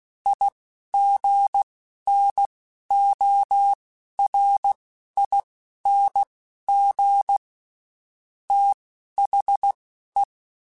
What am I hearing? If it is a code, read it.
Morse code, heard as a keyed tone, about 16 wpm, spaced as for 11 wpm, 800 Hz. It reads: IGNORING THE
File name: Quotes Spaced / Quotes Spaced-mp3s / IgnoringFacts sp @15